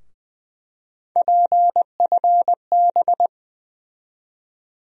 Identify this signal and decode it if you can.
Morse code, heard as a keyed tone, about 20 wpm, 700 Hz. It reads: PFB
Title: USA Call Sign Postfixes - 3 Letters 20wpm